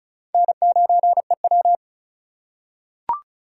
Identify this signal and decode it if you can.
Morse code, heard as a keyed tone, about 36 wpm, 700 Hz. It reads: N9EW
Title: USA Call Signs - 4 Characters - 2x Character Spacing - 35wpm